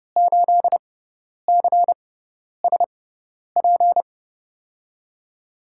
Morse code, heard as a keyed tone, about 30 wpm, spaced as for 10 wpm, 700 Hz. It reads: 8CSP
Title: USA Call Signs - 5 Characters - 6x Character Spacing - 30wpm